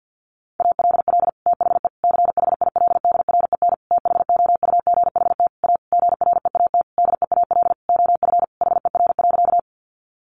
Morse code, heard as a keyed tone, about 50 wpm, 700 Hz. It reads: ALL THE CHILDREN THOUGHT A GREAT DEAL OF HER.
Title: Sentences from Top 800 Words 50wpm